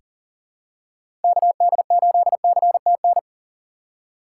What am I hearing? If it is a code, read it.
Morse code, heard as a keyed tone, about 40 wpm, 700 Hz. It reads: KD8CTN
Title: USA Call Signs - 3x Character Spacing - 40wpm